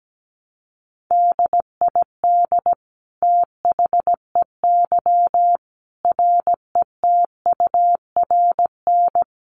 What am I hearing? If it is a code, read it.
Morse code, heard as a keyed tone, about 17 wpm, 700 Hz. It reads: DID THEY RETURN
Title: Sentences from Top 800 Words 17wpm